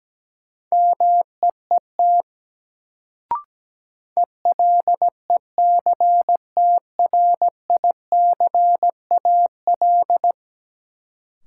Morse code, heard as a keyed tone, about 17 wpm, 700 Hz. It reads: MEET ELECTRICAL
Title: Sentences from Top 1000 Words - Review Part 6 - New Word Drill - 17wpm